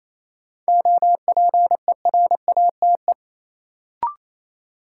Morse code, heard as a keyed tone, about 28 wpm, 700 Hz. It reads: OPERATE
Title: Sentences from Top 1500 Words - New Word Drill Alt - 28wpm